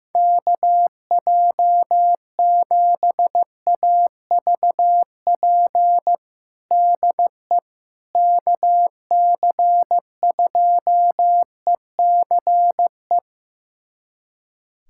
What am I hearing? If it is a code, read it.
Morse code, heard as a keyed tone, about 15 wpm, 700 Hz. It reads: KJ7AVP DE KC2ECE
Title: USA Callsign DE Callsign 15wpm